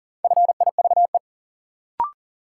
Morse code, heard as a keyed tone, about 38 wpm, 700 Hz. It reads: FIVE
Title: Sentences from Top 400 Words - New Word Drill - 40wpm